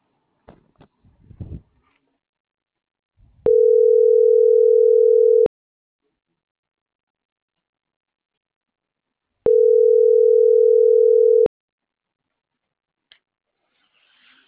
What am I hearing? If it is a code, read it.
Ringback tone (North America), 440 + 480 Hz.